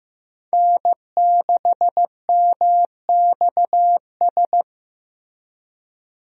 Morse code, heard as a keyed tone, about 16 wpm, 700 Hz. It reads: N6MXS
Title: USA Call Signs - 5 Characters - Repeated 3x - 15wpm